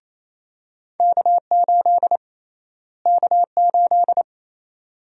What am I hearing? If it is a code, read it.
Morse code, heard as a keyed tone, about 28 wpm, 700 Hz. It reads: K8 K8
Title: USA Call Sign Prefixes - Repeated 2x - 28wpm